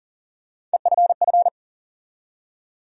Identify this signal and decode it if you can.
Morse code, heard as a keyed tone, about 40 wpm, 700 Hz. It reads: EFF
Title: USA Call Sign Postfixes - 3 Letters 40wpm